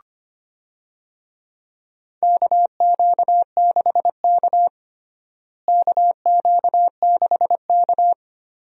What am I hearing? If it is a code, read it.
Morse code, heard as a keyed tone, about 25 wpm, 700 Hz. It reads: KQ6K KQ6K
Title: USA Call Signs - 4 Characters - Repeated 2x - 25wpm